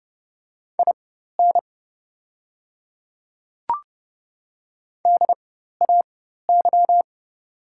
Morse code, heard as a keyed tone, about 30 wpm, spaced as for 14 wpm, 700 Hz. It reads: IN DAY
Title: 4x Character Spacing - Top 100 Words - 30wpm